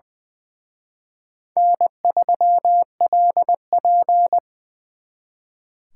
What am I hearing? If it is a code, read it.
Morse code, heard as a keyed tone, about 24 wpm, 700 Hz. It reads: N3LP
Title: Speed Racing - USA Call Signs 20wpm